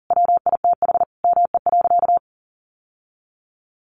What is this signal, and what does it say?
Morse code, heard as a keyed tone, about 40 wpm, 700 Hz. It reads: WITH ME.
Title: Sentences from Top 800 Words 40wpm